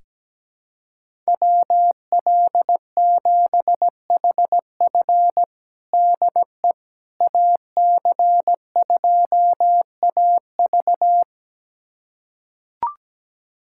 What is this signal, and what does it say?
Morse code, heard as a keyed tone, about 17 wpm, 700 Hz. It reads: WL7HF DE AC2AV
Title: USA Callsign DE Callsign - 5 Characters - 17wpm